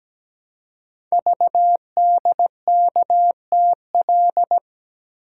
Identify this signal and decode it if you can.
Morse code, heard as a keyed tone, about 17 wpm, 700 Hz. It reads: VDKTL
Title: Code Groups - 5 Letters - 17wpm